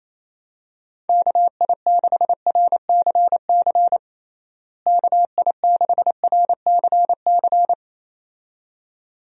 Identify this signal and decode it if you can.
Morse code, heard as a keyed tone, about 28 wpm, 700 Hz. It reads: KI6RCC KI6RCC
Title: USA Call Signs - 6 Characters - Repeated 2x - 28wpm